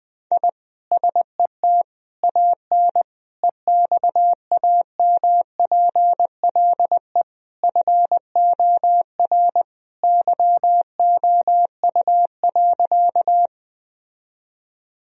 Morse code, heard as a keyed tone, about 20 wpm, 700 Hz. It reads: I SET AN EXAMPLE FOR YOU.
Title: Sentences from Top 1000 Words - Review Part 1 - 20wpm